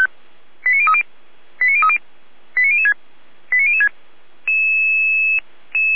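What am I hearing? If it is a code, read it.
Five-tone selective call (ZVEI), heard as quick runs of five tones, tones of about 70 ms. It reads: 89030 89030 89006 89006
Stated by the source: SDR-received ZVEI five-tone selective call.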